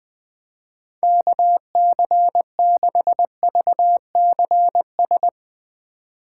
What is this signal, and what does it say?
Morse code, heard as a keyed tone, about 20 wpm, 700 Hz. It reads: KC6VCS
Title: Speed Racing - USA Call Signs 17wpm